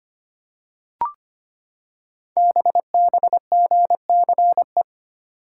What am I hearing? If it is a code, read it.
Morse code, heard as a keyed tone, about 25 wpm, 700 Hz. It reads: BBGCE
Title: Code Groups - 5 Letter:Numbers - 25wpm